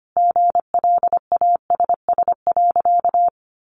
Morse code, heard as a keyed tone, about 25 wpm, 700 Hz. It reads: GLASS.
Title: Sentences from Top 1300 Words 25wpm